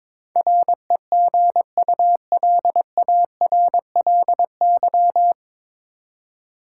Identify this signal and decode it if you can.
Morse code, heard as a keyed tone, about 22 wpm, 700 Hz. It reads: REGULARLY
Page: Morse Code Ninja practice 22wpm